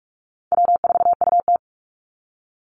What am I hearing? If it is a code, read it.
Morse code, heard as a keyed tone, about 44 wpm, 700 Hz. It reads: W4UT
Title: Call Signs - 45wpm